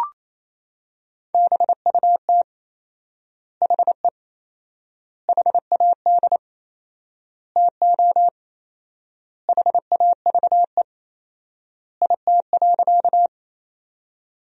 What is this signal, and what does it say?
Morse code, heard as a keyed tone, about 28 wpm, 700 Hz. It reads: BUT HE HAD TO HAVE IT.
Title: 4x Word-Spacing - Sentences from Top 100 Words - 28wpm